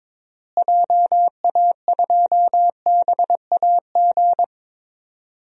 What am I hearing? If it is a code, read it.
Morse code, heard as a keyed tone, about 22 wpm, 700 Hz. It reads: JA2BAG
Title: DXCC-100 Call Signs - 6 Characters - Repeated 3x - 22wpm